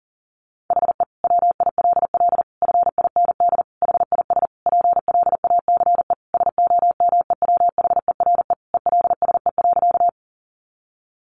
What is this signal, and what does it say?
Morse code, heard as a keyed tone, about 40 wpm, 700 Hz. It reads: HE WILL FIND HIS PLACE SOMEWHERE ELSE.
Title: Sentences from Top 1400 Words 40wpm